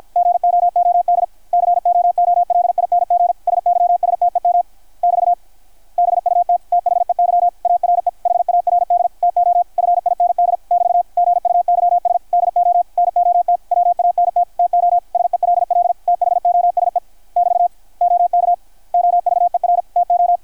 Morse code, heard as a keyed tone, about 51 wpm, 690 Hz, with radio noise. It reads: GOOD COOLING SYSTEM <BT> BUT THEY ARE HARD TO FIND <BT> GUYS DO NOT WANT TO SELL THOSE <BT> OK OVER TO
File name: archived / CRSnet